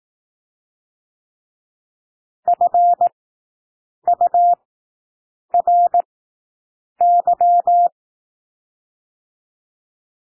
Morse code, heard as a keyed tone, about 18 wpm, spaced as for 7 wpm, 700 Hz. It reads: FURY
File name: SOUND WORDS 18 wpm